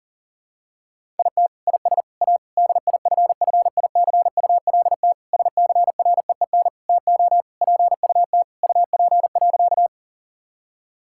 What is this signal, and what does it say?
Morse code, heard as a keyed tone, about 40 wpm, 700 Hz. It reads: IT IS A DIFFICULT SCREEN TO PUT UP.
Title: Sentences from Top 1000 Words - Review Part 6 - 40wpm